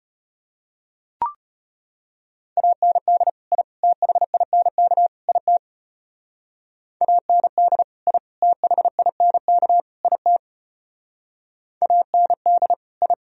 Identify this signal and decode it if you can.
Morse code, heard as a keyed tone, about 34 wpm, 700 Hz. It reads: AND I THINK IT AND I THINK IT AND I